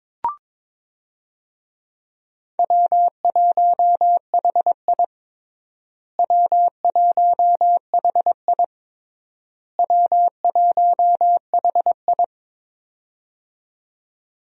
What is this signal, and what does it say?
Morse code, heard as a keyed tone, about 21 wpm, 700 Hz. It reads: W1HI W1HI W1HI
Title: USA Call Signs - 4 Characters - Repeated 3x - 22wpm